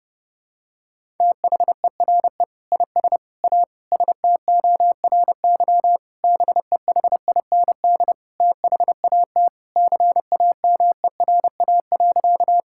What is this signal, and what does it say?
Morse code, heard as a keyed tone, about 32 wpm, 700 Hz. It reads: THERE IS A STORY BEHIND THAT CAMERA.
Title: Sentences from Top 1000 Words - Review Part 5 - 30wpm